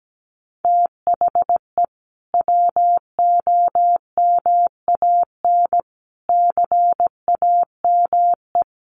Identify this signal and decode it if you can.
Morse code, heard as a keyed tone, about 17 wpm, 700 Hz. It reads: THE WOMAN CAME
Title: Sentences from Top 1100 Words 17wpm